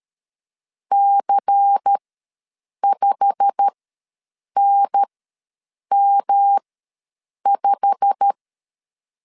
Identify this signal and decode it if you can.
Morse code, heard as a keyed tone, about 13 wpm, spaced as for 7 wpm, 800 Hz. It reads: C5NM5
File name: LEZIONI / 8 YJ45 / 8 YJ45 ws 12wpm 3146